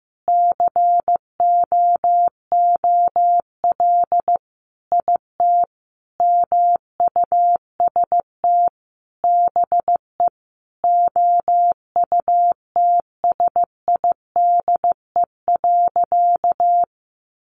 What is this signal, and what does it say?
Morse code, heard as a keyed tone, about 15 wpm, 700 Hz. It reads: COOL IT MUST BE OUTSIDE.
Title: Sentences from Top 1200 Words 15wpm